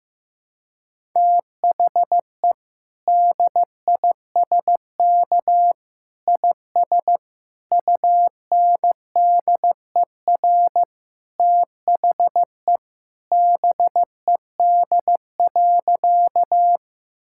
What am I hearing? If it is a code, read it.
Morse code, heard as a keyed tone, about 15 wpm, 700 Hz. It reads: THE DISK IS UNDER THE BED.
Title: Sentences from Top 1000 Words - Review Part 6 - 15wpm